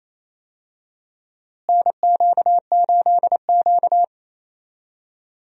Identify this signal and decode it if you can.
Morse code, heard as a keyed tone, about 28 wpm, 700 Hz. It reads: NQ8Q
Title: USA Call Signs - 4 Characters - Repeated 3x - 28wpm